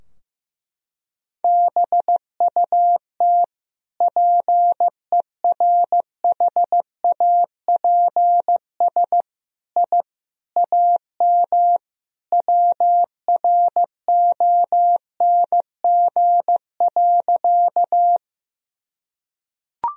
Morse code, heard as a keyed tone, about 15 wpm, 700 Hz. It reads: BUT PERHAPS I AM WRONG.
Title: Sentences from Top 1000 Words - Review Part 2 - 15wpm